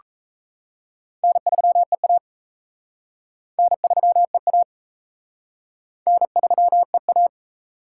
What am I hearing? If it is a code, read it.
Morse code, heard as a keyed tone, about 36 wpm, 700 Hz. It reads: N3EA N3EA N3EA